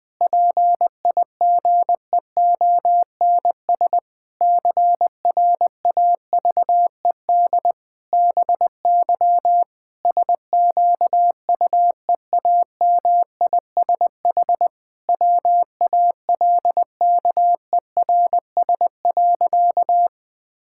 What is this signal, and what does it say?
Morse code, heard as a keyed tone, about 20 wpm, 700 Hz. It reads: PIGEONS CRAVED BY SQUEAMISH WALKERS.
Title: Pangram Challenge v2 - 20wpm